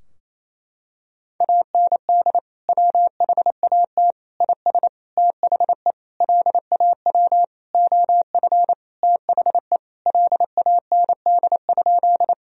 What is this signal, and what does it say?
Morse code, heard as a keyed tone, about 28 wpm, 700 Hz. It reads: AND WHAT IS THE LAW OF THE LAND?
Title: Sentences from Top 1000 Words - Review Part 3 - 28wpm